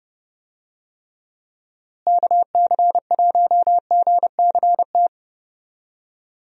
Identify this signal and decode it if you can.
Morse code, heard as a keyed tone, about 30 wpm, 700 Hz. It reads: KC1GCT
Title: USA Call Signs - 6 Characters - Repeated 3x - 30wpm